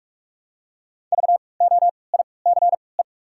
Morse code, heard as a keyed tone, about 45 wpm, 700 Hz. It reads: V O I C E
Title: 3x Character Spacing - Top 500 Words - 45wpm